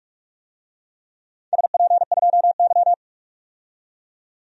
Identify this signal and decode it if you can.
Morse code, heard as a keyed tone, about 47 wpm, 700 Hz. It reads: SP2Y
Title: DXCC-100 Call Signs - 4 Characters - 45wpm